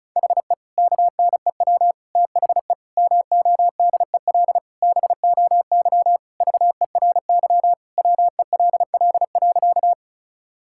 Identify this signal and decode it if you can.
Morse code, heard as a keyed tone, about 34 wpm, 700 Hz. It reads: HE KNEW THE MODEL BOY VERY WELL.
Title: Sentences from Top 1000 Words - Review Part 5 - 35wpm